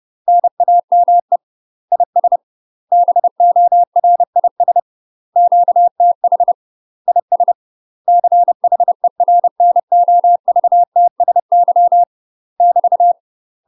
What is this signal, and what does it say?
Morse code, heard as a keyed tone, about 30 wpm, 700 Hz. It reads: NAME IS BORIS QTH IS CHERNOVTSY <BT>
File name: UK MASTER QSOs TRAINING / 30WPM / UK MASTER QSOs 0031